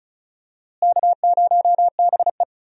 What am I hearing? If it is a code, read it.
Morse code, heard as a keyed tone, about 34 wpm, 700 Hz. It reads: K0BE
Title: USA Call Signs - 4 Characters - Repeated 3x - 35wpm